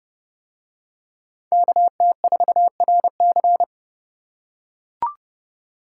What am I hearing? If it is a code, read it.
Morse code, heard as a keyed tone, about 30 wpm, 700 Hz. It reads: KT4RC